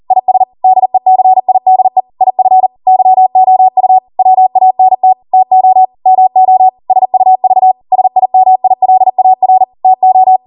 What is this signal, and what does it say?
Morse code, heard as a keyed tone, about 40 wpm, 750 Hz. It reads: IS DECIDE IF YOU WANT TO MO SUV SIMILAR TO